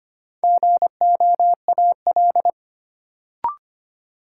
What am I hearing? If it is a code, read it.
Morse code, heard as a keyed tone, about 25 wpm, 700 Hz. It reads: GOAL
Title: Sentences from Top 1000 Words - Review Part 5 - New Word Drill - 25wpm